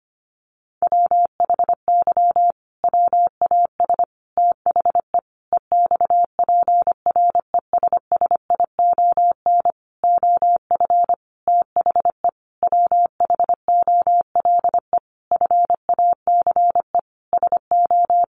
Morse code, heard as a keyed tone, about 25 wpm, 700 Hz. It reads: WHY WAS THE EXPRESSION OF THE WHOLE FACE SO
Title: Sentences from Top 1400 Words 25wpm